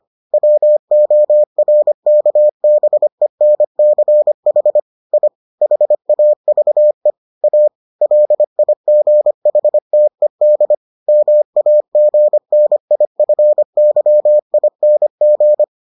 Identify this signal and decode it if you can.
Morse code, heard as a keyed tone, about 25 wpm, 600 Hz. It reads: WORKBENCH I HAVE A LIGHTED MAGNIFYING